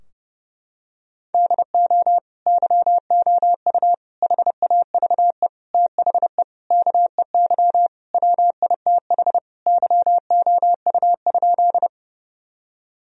Morse code, heard as a keyed tone, about 30 wpm, 700 Hz. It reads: DO YOU HAVE THE KEY WITH YOU?
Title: Sentences from Top 1000 Words - Review Part 4 - 30wpm